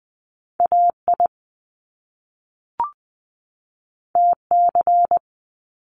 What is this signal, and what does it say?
Morse code, heard as a keyed tone, about 20 wpm, 700 Hz. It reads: A I T C
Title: Top Two Letter Combinations in Words 20wpm